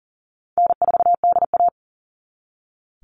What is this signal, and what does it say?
Morse code, heard as a keyed tone, about 40 wpm, 700 Hz. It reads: N4DA
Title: NA SOTA Chaser Callsigns 40wpm